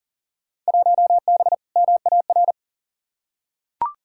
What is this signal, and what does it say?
Morse code, heard as a keyed tone, about 40 wpm, 700 Hz. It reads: 1B MAR
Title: Field Day Exchanges Encore - 40wpm